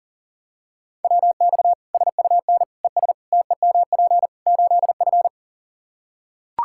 Morse code, heard as a keyed tone, about 40 wpm, 700 Hz. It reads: WX SUN ES TEMP 8F
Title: Practice QSO - Exchange 2 - Info - 40wpm